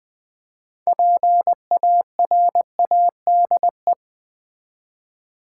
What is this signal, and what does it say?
Morse code, heard as a keyed tone, about 20 wpm, 700 Hz. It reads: PARADE
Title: 6 Letter Words - First 3 Letters Given - 20wpm